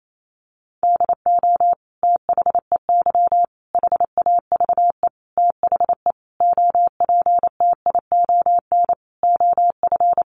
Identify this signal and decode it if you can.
Morse code, heard as a keyed tone, about 28 wpm, 700 Hz. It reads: DO THEY HAVE THE OPTION OF
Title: Sentences from Top 1300 Words 28wpm